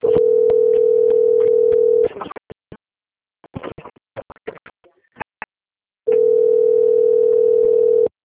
Ringback tone (North America), 440 + 480 Hz.